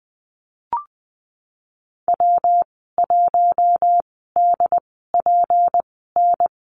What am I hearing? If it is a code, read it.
Morse code, heard as a keyed tone, about 20 wpm, spaced as for 15 wpm, 700 Hz. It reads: W1DPN
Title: 186 Lesson 30g - Letter b - Call Signs 20wpm